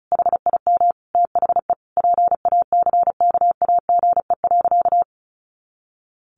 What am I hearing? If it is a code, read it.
Morse code, heard as a keyed tone, about 34 wpm, 700 Hz. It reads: HIM THE PACKAGE.
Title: Additional Easy Phrases 35wpm